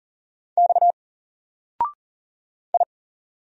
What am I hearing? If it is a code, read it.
Morse code, heard as a keyed tone, about 40 wpm, 700 Hz. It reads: X I
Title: Single Character (Letter:Number:Punctuation:Prosign) 40wpm